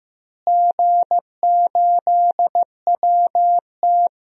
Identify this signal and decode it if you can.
Morse code, heard as a keyed tone, about 16 wpm, 700 Hz. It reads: G8WT
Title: DXCC-100 Call Signs - 4 Characters - Repeated 3x - 15wpm